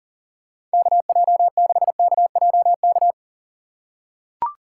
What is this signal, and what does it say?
Morse code, heard as a keyed tone, about 38 wpm, 690 Hz. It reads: KJ6KJK
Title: USA Call Signs - 6 Characters - 2x Character Spacing - 40wpm